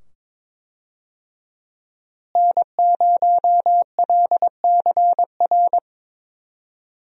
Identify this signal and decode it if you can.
Morse code, heard as a keyed tone, about 21 wpm, 700 Hz. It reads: N0LCR
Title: USA Call Signs - 5 Characters - Repeated 2x - 22wpm